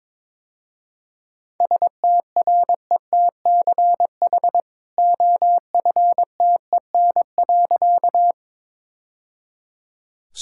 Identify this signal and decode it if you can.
Morse code, heard as a keyed tone, about 21 wpm, 690 Hz. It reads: STRETCH OFTEN.
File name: practice / 22wpm / Easy Sentences with Hints - 22wpm